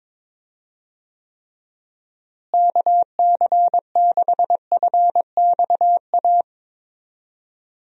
Morse code, heard as a keyed tone, about 22 wpm, 700 Hz. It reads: KC6FXA
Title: USA Call Signs - 6 Characters - Repeated 2x - 22wpm